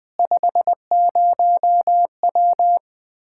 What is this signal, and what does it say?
Morse code, heard as a keyed tone, about 20 wpm, 700 Hz. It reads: 50W
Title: Practice QSO - Exchange 2 - Info - 20wpm